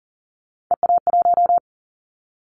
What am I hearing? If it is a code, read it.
Morse code, heard as a keyed tone, about 40 wpm, 700 Hz. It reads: EA1
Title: Callsign Prefixes for Top 25 DXCC - 40wpm